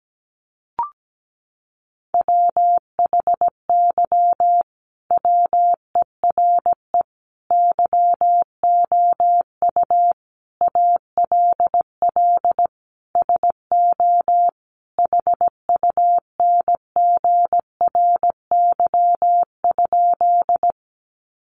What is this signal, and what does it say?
Morse code, heard as a keyed tone, about 17 wpm, 700 Hz. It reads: WHY WERE YOU ALL SO HUNGRY?
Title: Sentences from Top 1400 Words 17wpm